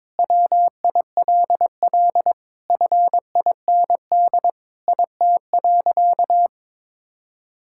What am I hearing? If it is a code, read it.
Morse code, heard as a keyed tone, about 23 wpm, 700 Hz. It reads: WILL FIND IT.
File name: practice / 22wpm / Sentences with 4-Word Phrases - 22wpm